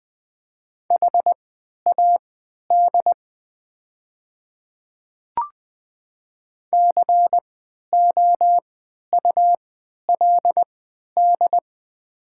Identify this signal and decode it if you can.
Morse code, heard as a keyed tone, about 20 wpm, spaced as for 11 wpm, 700 Hz. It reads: HAD COULD
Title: 3x Character Spacing - Top 500 Words - 20wpm